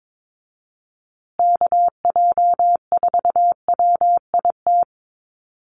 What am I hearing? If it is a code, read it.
Morse code, heard as a keyed tone, about 22 wpm, 700 Hz. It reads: KJ4WIT
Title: USA Call Signs Repeated 3x 22wpm